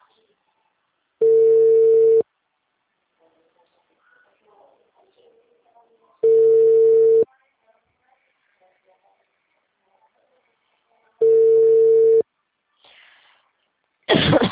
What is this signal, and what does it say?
Ringback tone (China, 450 Hz), with speech.